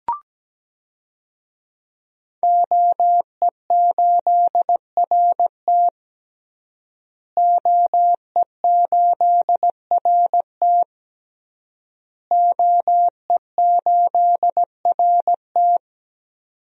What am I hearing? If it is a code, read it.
Morse code, heard as a keyed tone, about 17 wpm, 700 Hz. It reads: OE8RT OE8RT OE8RT
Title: DXCC-100 Call Signs - 5 Characters - Repeated 3x - 17wpm